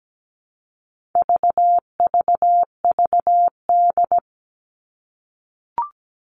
Morse code, heard as a keyed tone, about 17 wpm, 700 Hz. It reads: VVVD
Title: All Cut Numbers - 4-Digit 17wpm